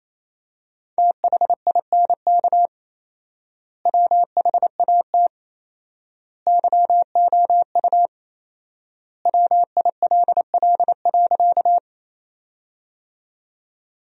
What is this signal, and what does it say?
Morse code, heard as a keyed tone, about 28 wpm, 700 Hz. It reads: THINK WHAT YOU WILL.
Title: 4x Word-Spacing - Sentences from Top 200 Words - 28wpm